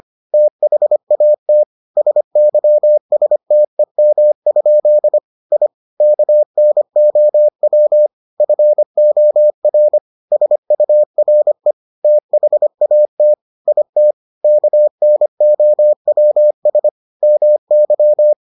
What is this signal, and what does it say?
Morse code, heard as a keyed tone, about 25 wpm, 600 Hz. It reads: THAT SYSTEM? I KNOW FOR SURE THAT IT KNOWS MY